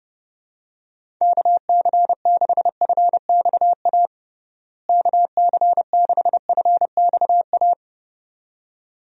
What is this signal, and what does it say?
Morse code, heard as a keyed tone, about 32 wpm, 700 Hz. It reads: KC6FXA KC6FXA
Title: USA Call Signs - 6 Characters - Repeated 2x - 30wpm